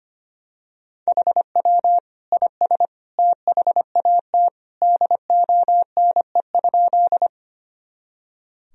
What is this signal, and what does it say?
Morse code, heard as a keyed tone, about 25 wpm, 700 Hz. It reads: HW IS THAT DONE?
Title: Sentences from Top 500 Words and Abbreviations 25wpm